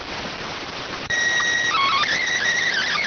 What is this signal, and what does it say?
SSTV (slow-scan television), PD 290.